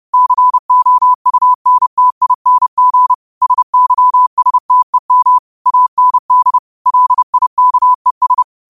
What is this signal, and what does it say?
Morse code, heard as a keyed tone, about 30 wpm, 1000 Hz. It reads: COUNTING SYSTEM AND LIKES
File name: qotdfiles / 30wpm